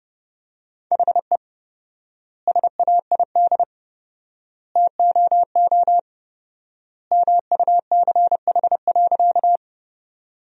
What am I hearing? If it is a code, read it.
Morse code, heard as a keyed tone, about 30 wpm, 700 Hz. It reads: HE SAID TOO MUCH.
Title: 4x Word-Spacing - Sentences from Top 200 Words - 30wpm